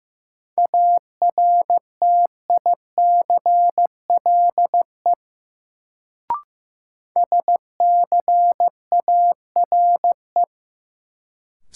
Morse code, heard as a keyed tone, about 15 wpm, 700 Hz. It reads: ARTICLE SCARE
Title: Sentences from Top 1000 Words - Review Part 5 - New Word Drill - 15wpm